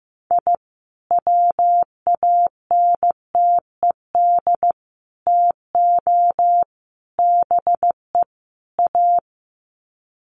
Morse code, heard as a keyed tone, about 15 wpm, 700 Hz. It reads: I WANTED TO BE A
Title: Sets of 5 Words Encore2 15wpm